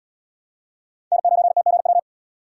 Morse code, heard as a keyed tone, about 77 wpm, 700 Hz. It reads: N9TGW